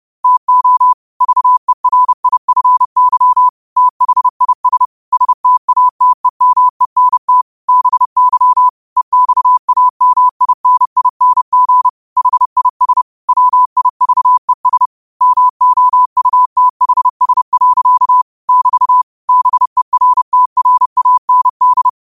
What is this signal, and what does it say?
Morse code, heard as a keyed tone, about 30 wpm, 1000 Hz. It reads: TO VERIFY THIS STATEMENT BY EXAMINING HIS WIVES MOUTHS. <BT> BERTRAND
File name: qotdfiles / 30wpm